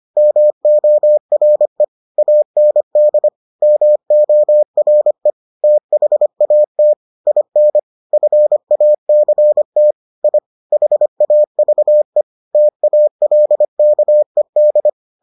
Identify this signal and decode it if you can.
Morse code, heard as a keyed tone, about 25 wpm, 600 Hz. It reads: MORE AND MORE THAT IN FACT I HAVE TALKED